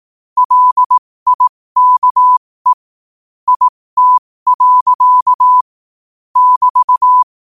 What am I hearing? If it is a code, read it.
Morse code, heard as a keyed tone, about 18 wpm, spaced as for 15 wpm, 1000 Hz. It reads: LIKE IT. <BT>
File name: qotdfiles / 15wpm